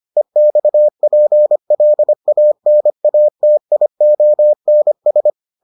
Morse code, heard as a keyed tone, about 25 wpm, 600 Hz. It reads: EXPLANATIONS